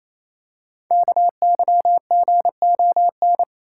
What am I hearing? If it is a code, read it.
Morse code, heard as a keyed tone, about 29 wpm, 700 Hz. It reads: KYGON
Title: Code Groups - 5 Letters - 28wpm